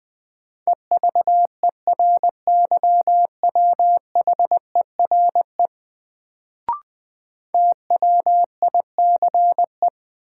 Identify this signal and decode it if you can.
Morse code, heard as a keyed tone, about 20 wpm, 700 Hz. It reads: EVERYWHERE TWICE
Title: Sentences from Top 1600 Words - New Word Drill Alt - 20wpm